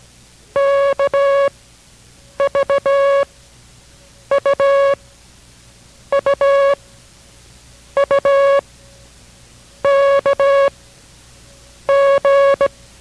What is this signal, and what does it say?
Morse code, heard as a keyed tone, about 17 wpm, 560 Hz. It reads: K V U U U K G